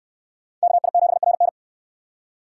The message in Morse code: KI7RN

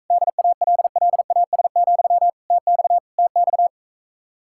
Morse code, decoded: DALLAS, TX TX

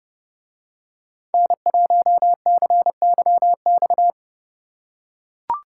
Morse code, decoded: N1CYX

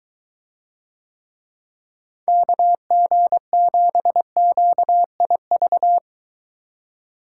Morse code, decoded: KG7QIV

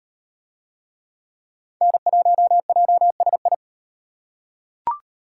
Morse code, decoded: N1JSI

